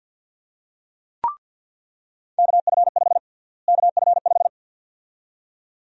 Morse code, heard as a keyed tone, about 50 wpm, 700 Hz. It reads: KF5 KF5
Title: USA Call Sign Prefixes - Repeated 2x - 50wpm